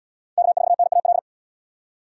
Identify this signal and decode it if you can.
Morse code, heard as a keyed tone, about 73 wpm, 700 Hz. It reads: K4NAL